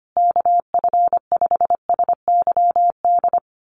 Morse code, heard as a keyed tone, about 25 wpm, 700 Hz. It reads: KF5SYD
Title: USA Call Signs Repeated 2x 25wpm